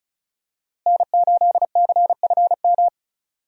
Morse code, heard as a keyed tone, about 36 wpm, 700 Hz. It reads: N8CFM
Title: USA Call Signs - 5 Characters - Repeated 3x - 35wpm